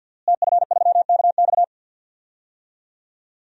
Morse code, heard as a keyed tone, about 47 wpm, 700 Hz. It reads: TF3KX
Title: DXCC-100 Call Signs - 5 Characters - Repeated 2x - 50wpm